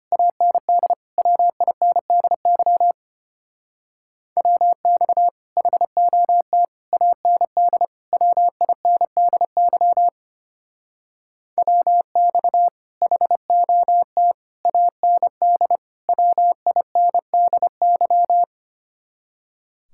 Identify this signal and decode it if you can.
Morse code, heard as a keyed tone, about 31 wpm, 700 Hz. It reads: AND WINDY WX HOT AND WINDY WX HOT AND WINDY
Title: Speed-Racing Common QSO Phrases - 25wpm